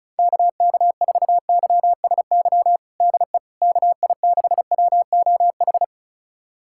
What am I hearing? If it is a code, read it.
Morse code, heard as a keyed tone, about 36 wpm, 690 Hz. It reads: KK4YSY DE KI6WOH
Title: USA Callsign DE Callsign 35wpm